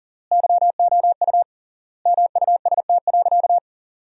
Morse code, heard as a keyed tone, about 40 wpm, 700 Hz. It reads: YOU MUST.
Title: 3x Word-Spacing - Sentences from Top 200 Words - 40wpm